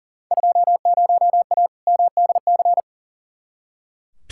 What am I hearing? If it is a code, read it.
Morse code, heard as a keyed tone, about 39 wpm, 700 Hz. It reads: 20A MDC
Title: Field Day Exchanges Encore - 40wpm